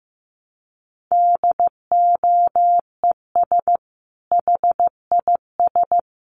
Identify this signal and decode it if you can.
Morse code, heard as a keyed tone, about 15 wpm, 700 Hz. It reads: DOES HIS
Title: Sentences from Top 1500 Words 15wpm